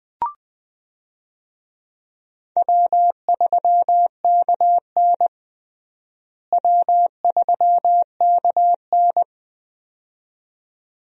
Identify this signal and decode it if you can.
Morse code, heard as a keyed tone, about 20 wpm, 700 Hz. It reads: W3KN W3KN
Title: USA Call Signs - 4 Characters - Repeated 2x - 20wpm